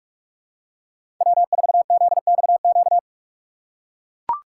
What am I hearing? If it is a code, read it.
Morse code, heard as a keyed tone, about 43 wpm, 700 Hz. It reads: W4ZXQ
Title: USA Call Signs - 6x Character Spacing - 45wpm